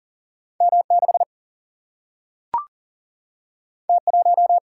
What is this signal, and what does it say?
Morse code, heard as a keyed tone, about 38 wpm, 700 Hz. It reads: M6 T1
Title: Single-Letter Single-Number 40wpm